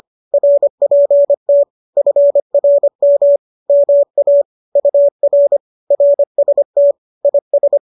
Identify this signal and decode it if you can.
Morse code, heard as a keyed tone, about 25 wpm, 570 Hz. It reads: RPT FRM MA UR RST IS